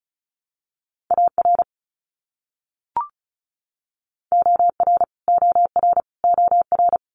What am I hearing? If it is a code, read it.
Morse code, heard as a keyed tone, about 34 wpm, 700 Hz. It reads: AR OR OR OR
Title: US States Repeated 3x 35wpm